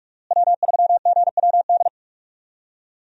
Morse code, heard as a keyed tone, about 44 wpm, 700 Hz. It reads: W3GWD
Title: USA Call Signs - 5 Characters - Repeated 2x - 45wpm